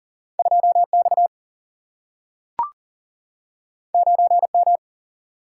Morse code, heard as a keyed tone, about 42 wpm, 700 Hz. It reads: 2 X 9 M
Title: Single-Letter Single-Number 40wpm